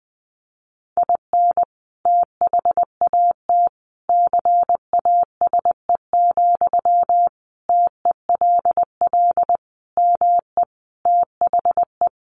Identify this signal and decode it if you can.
Morse code, heard as a keyed tone, about 20 wpm, 700 Hz. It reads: IN THAT CASE, TELL ME THE
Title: Sentences from Top 1400 Words 20wpm